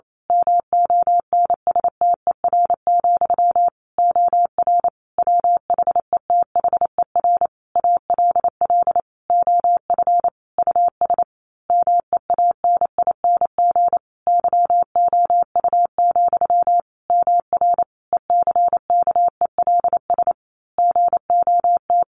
Morse code, heard as a keyed tone, about 28 wpm, 700 Hz. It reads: MONSTER, OR WHETHER ALL OF US MEANING YOU, MR ECKELS GOT
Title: A Sound Of Thunder 28wpm